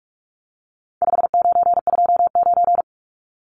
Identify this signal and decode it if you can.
Morse code, heard as a keyed tone, about 44 wpm, 700 Hz. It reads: 5929
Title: Numbers - 4-Digit 45wpm